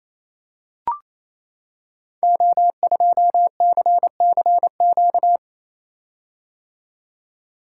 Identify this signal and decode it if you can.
Morse code, heard as a keyed tone, about 29 wpm, 700 Hz. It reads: O2CCQ
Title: Code Groups - 5 Letter:Numbers - 28wpm